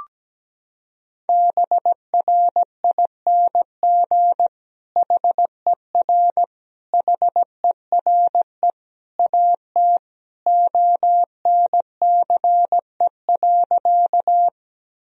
Morse code, heard as a keyed tone, about 17 wpm, 700 Hz. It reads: BRING HER HERE AT ONCE.
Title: Sentences from Top 1000 Words - Review Part 3 - 17wpm